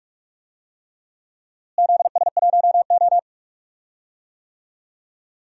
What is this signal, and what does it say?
Morse code, heard as a keyed tone, about 41 wpm, 700 Hz. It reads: ZS1O
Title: DXCC-100 Call Signs - 4 Characters - Repeated 3x - 45wpm